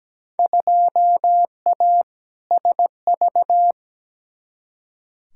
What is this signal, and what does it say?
Morse code, heard as a keyed tone, about 17 wpm, 700 Hz. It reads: 2A SV
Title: Field Day Exchanges - 17wpm